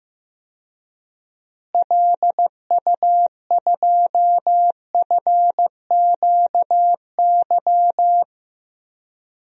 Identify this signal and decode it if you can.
Morse code, heard as a keyed tone, about 15 wpm, 700 Hz. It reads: LU2FQY
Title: DXCC-100 Call Signs - 6 Characters - Repeated 2x - 15wpm